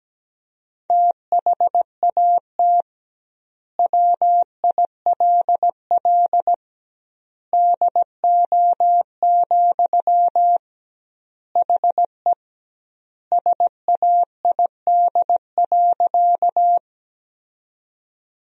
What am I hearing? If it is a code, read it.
Morse code, heard as a keyed tone, about 17 wpm, 700 Hz. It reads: THAT WILL DO, HE SAID.
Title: 2x Word-Spacing - Sentences from Top 100 Words - 17wpm